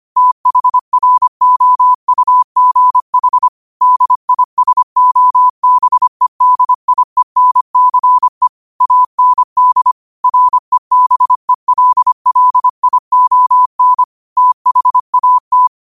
Morse code, heard as a keyed tone, about 25 wpm, 1000 Hz. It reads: THROUGH DISOBEDIENCE AND REBELLION THAT
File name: qotdfiles / 25wpm